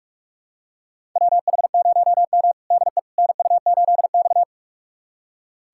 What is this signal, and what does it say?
Morse code, heard as a keyed tone, about 41 wpm, 700 Hz. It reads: WH0M DE NU8X
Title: USA Callsign DE Callsign - 4 Characters - 45wpm